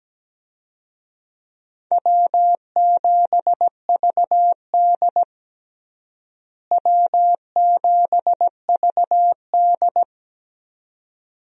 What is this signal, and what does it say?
Morse code, heard as a keyed tone, about 17 wpm, 700 Hz. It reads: W7VD W7VD
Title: USA Call Signs - 4 Characters - Repeated 2x - 17wpm